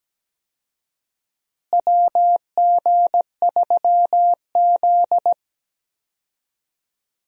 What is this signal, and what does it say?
Morse code, heard as a keyed tone, about 17 wpm, 700 Hz. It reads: WG3Z
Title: USA Call Signs - 4 Characters - Repeated 2x - 17wpm